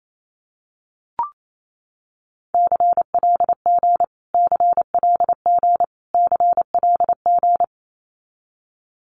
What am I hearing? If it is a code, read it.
Morse code, heard as a keyed tone, about 28 wpm, 700 Hz. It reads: CLG CLG CLG
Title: Abbreviations Repeated 3x - 28wpm